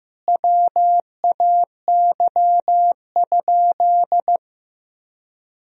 Morse code, heard as a keyed tone, about 15 wpm, 700 Hz. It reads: WAY?
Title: Sentences from Top 200 Words and Abbreviations 15wpm